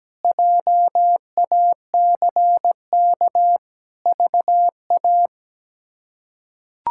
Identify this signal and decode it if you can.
Morse code, heard as a keyed tone, about 17 wpm, 700 Hz. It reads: JACK VA
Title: K1USN SST QSO Practice - 17wpm